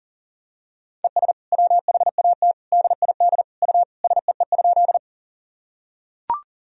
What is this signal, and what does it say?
Morse code, heard as a keyed tone, about 40 wpm, 700 Hz. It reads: ES WHAT DID U SEE?
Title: Sentences from Top 100 Words and Abbreviations 40wpm